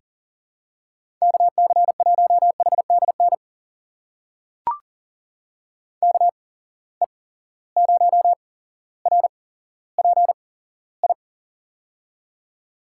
Morse code, heard as a keyed tone, about 40 wpm, spaced as for 17 wpm, 700 Hz. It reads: KC1HDN KE0RPI
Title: USA Call Signs - 8x Character Spacing - 40wpm